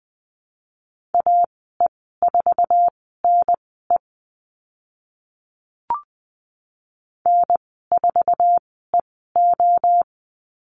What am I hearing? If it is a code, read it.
Morse code, heard as a keyed tone, about 20 wpm, spaced as for 14 wpm, 700 Hz. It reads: AE4NE N4EO
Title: 076 Lesson 13g - Letter L - Call Signs 20wpm